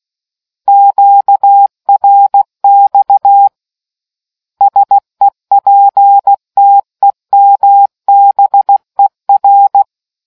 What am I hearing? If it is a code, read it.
Morse code, heard as a keyed tone, about 16 wpm, 800 Hz. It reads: QRX SEPTEMBER